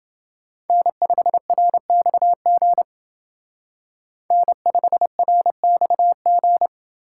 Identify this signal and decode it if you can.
Morse code, heard as a keyed tone, about 24 wpm, 700 Hz. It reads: N5RXG N5RXG
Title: Speed Racing - USA 1x3 Call Signs 20wpm